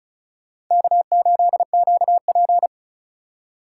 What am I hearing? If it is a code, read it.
Morse code, heard as a keyed tone, about 48 wpm, spaced as for 42 wpm, 700 Hz. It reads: K8QP
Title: Speed Racing - USA 1x2 Call Signs 35wpm